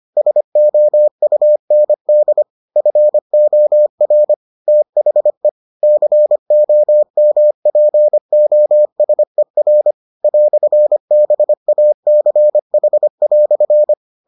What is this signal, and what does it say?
Morse code, heard as a keyed tone, about 25 wpm, 600 Hz. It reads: SOUND FOR THE COMPOSER "BACH"